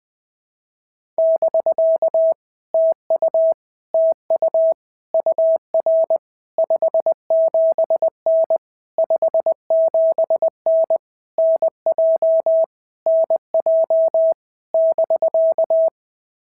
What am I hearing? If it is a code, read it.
Morse code, heard as a keyed tone, about 20 wpm, 650 Hz. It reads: <BK> TU TU UR 57N 57N NJ NJ <BK>
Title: POTA QSO Practice - 20wpm